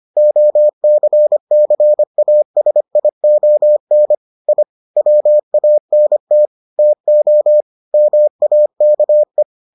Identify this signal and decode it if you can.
Morse code, heard as a keyed tone, about 25 wpm, 600 Hz. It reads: OCCASION I WANT TO MAKE